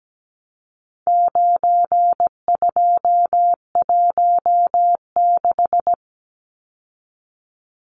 Morse code, heard as a keyed tone, about 17 wpm, 700 Hz. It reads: 9216
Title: Numbers - 4-Digit 17wpm